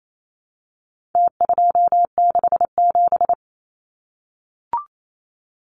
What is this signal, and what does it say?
Morse code, heard as a keyed tone, about 28 wpm, 700 Hz. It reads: T267
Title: Numbers - Contest Style - 4-Digit 28wpm